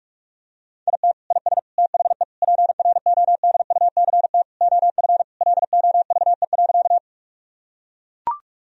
Morse code, heard as a keyed tone, about 42 wpm, 700 Hz. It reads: IT IS THE PRODUCT OF LOVE.
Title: Sentences from Top 1000 Words - Review Part 2 - 45wpm